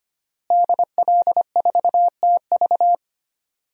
Morse code, heard as a keyed tone, about 25 wpm, 700 Hz. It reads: DL4TV
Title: DXCC-100 Call Signs - 5 Characters - Repeated 3x - 25wpm